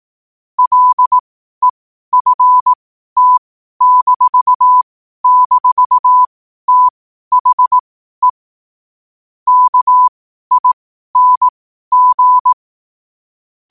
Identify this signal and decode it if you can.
Morse code, heard as a keyed tone, about 18 wpm, spaced as for 12 wpm, 1000 Hz. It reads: LEFT--THE KING
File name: qotdfiles / 12wpm